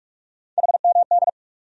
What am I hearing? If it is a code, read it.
Morse code, heard as a keyed tone, about 43 wpm, 700 Hz. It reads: HMD
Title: USA Call Sign Postfixes - 3 Letters 45wpm